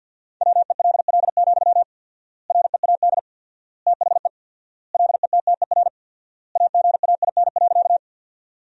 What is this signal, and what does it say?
Morse code, heard as a keyed tone, about 50 wpm, 700 Hz. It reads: WELL, READ THE LETTER AGAIN.
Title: 4x Word-Spacing - Sentences from Top 300 Words - 50wpm